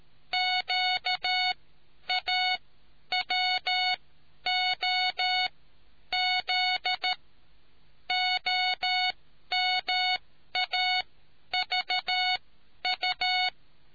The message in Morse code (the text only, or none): QAWOZ OMAVU